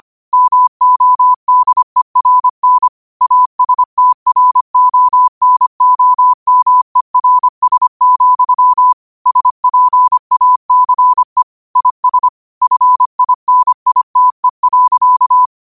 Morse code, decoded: MODERN ASTRONOMERS, SPACE IS FINITE.

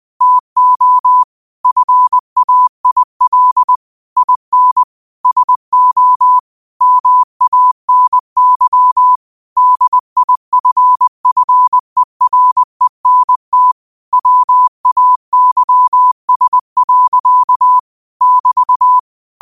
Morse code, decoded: TO FAIL IN SO MANY DIFFERENT WAYS. <BT>

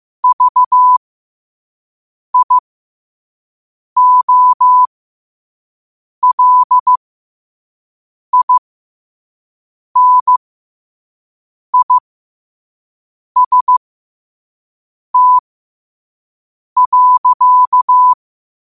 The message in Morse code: VIOLINIST.

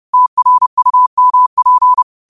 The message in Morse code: TRUMP

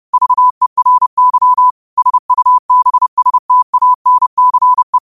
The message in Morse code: VERY SUBSTANCE

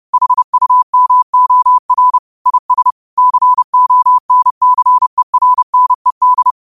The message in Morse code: HUMOR IS CONCERNED

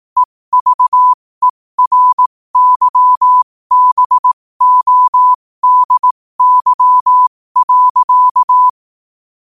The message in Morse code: EVERYBODY.

